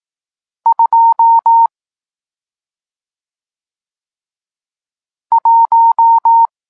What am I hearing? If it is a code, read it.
Morse code, heard as a keyed tone, about 18 wpm, 920 Hz. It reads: 2 1
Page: Tekst 40 – Nauka znaków przestankowych